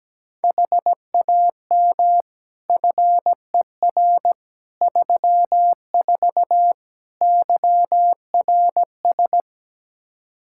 Morse code, heard as a keyed tone, about 17 wpm, 700 Hz. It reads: HAM FER 34 YRS
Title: Practice QSO - Exchange 3 - More Info - 17wpm